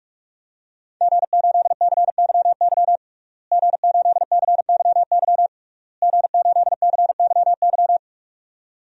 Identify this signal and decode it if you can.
Morse code, heard as a keyed tone, about 46 wpm, 700 Hz. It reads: G8CYY G8CYY G8CYY
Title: DXCC-100 Call Signs - 5 Characters - Repeated 3x - 45wpm